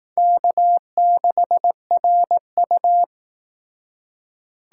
Morse code, heard as a keyed tone, about 17 wpm, 700 Hz. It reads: K6RU
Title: Speed Racing - USA 1x2 Call Signs 15wpm